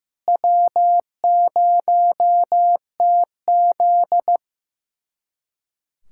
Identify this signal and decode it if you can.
Morse code, heard as a keyed tone, about 15 wpm, 700 Hz. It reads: W0TZ
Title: USA Callsign DE Callsign - 4 Characters - 15wpm